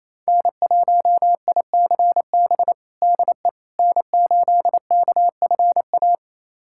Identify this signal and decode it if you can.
Morse code, heard as a keyed tone, about 28 wpm, 700 Hz. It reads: N1ICB DE N8KFA
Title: USA Callsign DE Callsign - 5 Characters - 28wpm